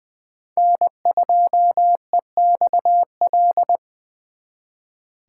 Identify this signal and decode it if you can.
Morse code, heard as a keyed tone, about 23 wpm, 700 Hz. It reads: N2EXL
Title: Speed Racing - USA 1x3 Call Signs 20wpm